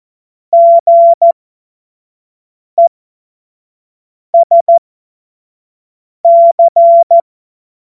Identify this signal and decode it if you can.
Morse code, heard as a keyed tone, about 14 wpm, spaced as for 5 wpm, 680 Hz. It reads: GESC